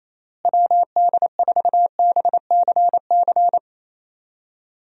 Morse code, heard as a keyed tone, about 28 wpm, 700 Hz. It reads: WD4BCC
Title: USA Call Signs - 6 Characters - Repeated 3x - 28wpm